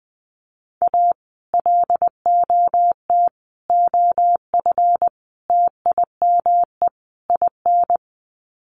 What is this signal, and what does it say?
Morse code, heard as a keyed tone, about 20 wpm, 700 Hz. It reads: A LOT OF TIME IN
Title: Sets of 5 Words Encore2 20wpm